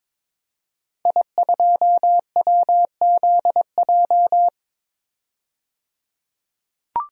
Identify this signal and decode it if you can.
Morse code, heard as a keyed tone, about 22 wpm, 700 Hz. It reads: I2WZJ
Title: DXCC-100 Call Signs - 5 Characters - 22wpm